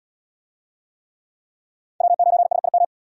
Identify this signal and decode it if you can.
Morse code, heard as a keyed tone, about 73 wpm, 700 Hz. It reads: W9SIM